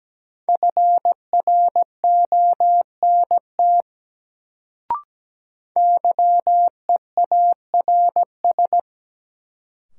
Morse code, heard as a keyed tone, about 17 wpm, 700 Hz. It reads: FRONT YEARS
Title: Sets of 2 Words - New Word Drill - 17wpm